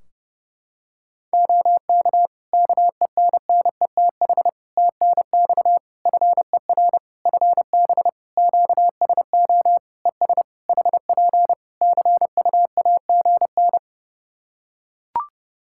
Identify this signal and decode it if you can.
Morse code, heard as a keyed tone, about 30 wpm, 700 Hz. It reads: OK KENNETH TNX FER FB QSO ES HP CUAGN
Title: Practice QSO - Exchange 4 - Ending - 30wpm